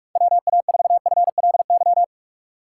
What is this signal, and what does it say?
Morse code, heard as a keyed tone, about 43 wpm, 700 Hz. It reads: WA4FLY
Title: USA Call Signs - 2x Character Spacing - 45wpm